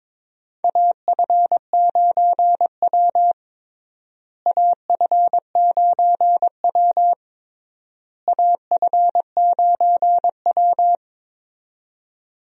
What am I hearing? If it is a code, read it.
Morse code, heard as a keyed tone, about 22 wpm, 700 Hz. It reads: AF9W AF9W AF9W
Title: USA Call Signs - 4 Characters - Repeated 3x - 22wpm